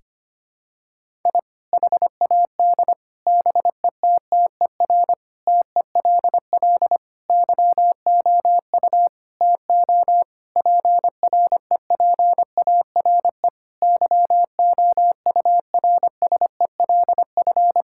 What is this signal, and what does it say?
Morse code, heard as a keyed tone, about 25 wpm, 700 Hz. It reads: I HAD BETTER TELL YOU TO PREPARE YOURSELF
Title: Sentences from Top 1000 Words - Review Part 6 - 25wpm